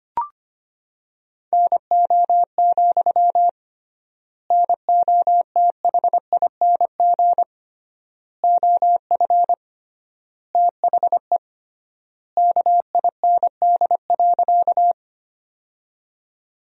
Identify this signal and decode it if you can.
Morse code, heard as a keyed tone, about 25 wpm, 700 Hz. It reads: NO, NOTHING OF THE KIND.
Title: 3x Word-Spacing - Sentences from Top 500 Words - 25wpm